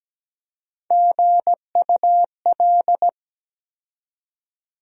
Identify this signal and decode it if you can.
Morse code, heard as a keyed tone, about 17 wpm, 700 Hz. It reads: GUL
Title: USA Call Sign Postfixes - 3 Letters 17wpm